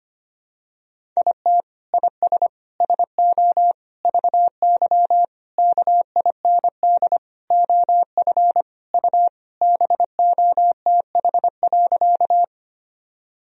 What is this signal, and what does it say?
Morse code, heard as a keyed tone, about 25 wpm, 700 Hz. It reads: IT IS SO VY KIND OF U BOTH.
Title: Sentences from Top 300 Words and Abbreviations 25wpm